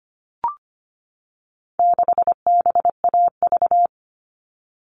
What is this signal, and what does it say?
Morse code, heard as a keyed tone, about 25 wpm, 700 Hz. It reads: 6BAV
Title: All Cut Numbers - 4-Digit 25wpm